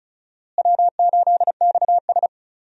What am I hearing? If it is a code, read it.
Morse code, heard as a keyed tone, about 36 wpm, 700 Hz. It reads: W8XS
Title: USA Call Signs - 4 Characters - Repeated 3x - 35wpm